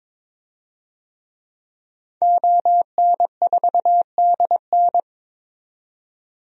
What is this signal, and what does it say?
Morse code, heard as a keyed tone, about 22 wpm, 700 Hz. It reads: ON4DN
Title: DXCC-100 Call Signs - 5 Characters - 22wpm